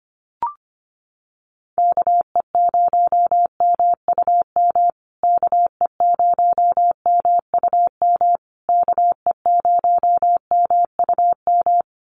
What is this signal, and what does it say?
Morse code, heard as a keyed tone, about 25 wpm, 700 Hz. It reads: KE0MUM KE0MUM KE0MUM
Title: USA Call Signs Repeated 3x 25wpm